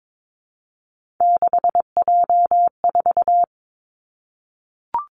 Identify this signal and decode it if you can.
Morse code, heard as a keyed tone, about 22 wpm, 700 Hz. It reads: 6J4
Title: Three Challenging Letter-Number Combinations 22wpm